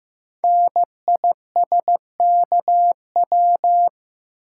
Morse code, heard as a keyed tone, about 15 wpm, 700 Hz. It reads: NISKW
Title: Code Groups - 5 Letters - 15wpm